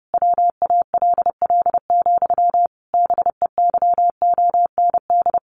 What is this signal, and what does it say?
Morse code, heard as a keyed tone, about 30 wpm, 700 Hz. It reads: WALL, BEYOND
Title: A Sound Of Thunder 30wpm